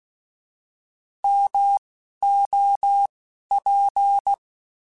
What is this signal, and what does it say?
Morse code, heard as a keyed tone, about 16 wpm, 800 Hz. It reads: M O P